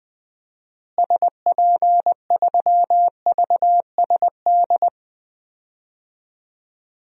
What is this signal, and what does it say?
Morse code, heard as a keyed tone, about 20 wpm, 700 Hz. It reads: SP3VSD
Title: DXCC-100 Call Signs - 6 Characters - Repeated 3x - 20wpm